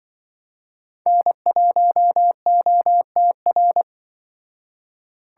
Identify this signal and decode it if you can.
Morse code, heard as a keyed tone, about 24 wpm, 700 Hz. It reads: N1OTR
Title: Speed Racing - USA 1x3 Call Signs 20wpm